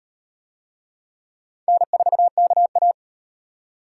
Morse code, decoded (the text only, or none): N4KA